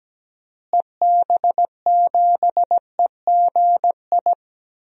Morse code, heard as a keyed tone, about 17 wpm, 700 Hz. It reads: EB7EGI
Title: DXCC-100 Call Signs - 6 Characters - Repeated 3x - 17wpm